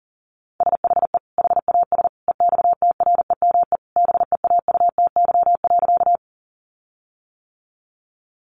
Morse code, heard as a keyed tone, about 40 wpm, 700 Hz. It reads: SHE HAS EXTREME BEAUTY.
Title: Sentences from Top 1300 Words 40wpm